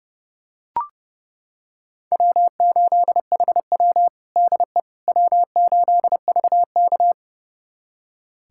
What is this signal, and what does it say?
Morse code, heard as a keyed tone, about 29 wpm, 700 Hz. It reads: W8HW DE W8VK
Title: USA Callsign DE Callsign - 4 Characters - 30wpm